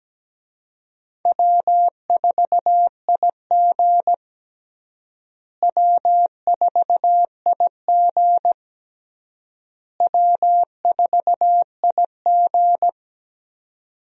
Morse code, heard as a keyed tone, about 17 wpm, 700 Hz. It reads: W4IG W4IG W4IG
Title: USA Call Signs - 4 Characters - Repeated 3x - 17wpm